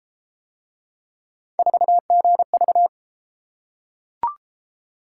Morse code, heard as a keyed tone, about 32 wpm, 700 Hz. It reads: 4GV